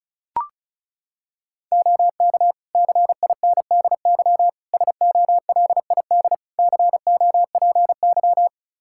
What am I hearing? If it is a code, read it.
Morse code, heard as a keyed tone, about 36 wpm, 700 Hz. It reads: OK CINDY SOLID COPY
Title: K1USN SST QSO Practice 35wpm